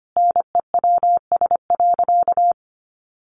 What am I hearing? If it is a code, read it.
Morse code, heard as a keyed tone, about 25 wpm, 700 Hz. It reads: NEWS.
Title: Sentences from Top 1100 Words 25wpm